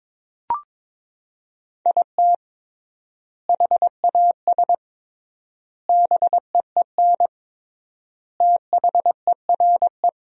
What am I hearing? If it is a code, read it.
Morse code, heard as a keyed tone, about 22 wpm, 700 Hz. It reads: IT HAS BEEN THERE
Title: 3x Word-Spacing - Sentences from Top 400 Words - 22wpm